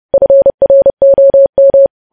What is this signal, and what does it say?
Morse code, heard as a keyed tone, about 30 wpm, 550 Hz. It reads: FROM
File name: podcast / News headlines-1x-30x30